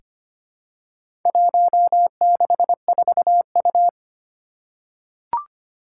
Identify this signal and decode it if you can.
Morse code, heard as a keyed tone, about 25 wpm, 700 Hz. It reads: 164U
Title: Four Challenging Letter-Number Combinations 25wpm